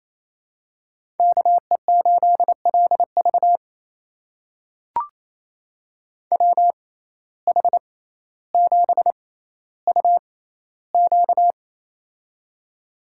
Morse code, heard as a keyed tone, about 28 wpm, spaced as for 14 wpm, 700 Hz. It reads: KE8LV WH7UQ
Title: USA Call Signs - 5 Characters - 6x Character Spacing - 28wpm